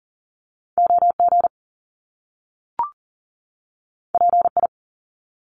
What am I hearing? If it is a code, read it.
Morse code, heard as a keyed tone, about 38 wpm, 700 Hz. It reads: OG PI
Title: Top Two Letter Combinations in Words 40wpm